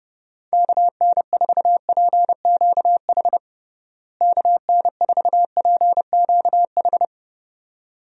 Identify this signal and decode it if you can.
Morse code, heard as a keyed tone, about 30 wpm, 700 Hz. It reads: KN4PQH KN4PQH
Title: USA Call Signs - 6 Characters - Repeated 2x - 30wpm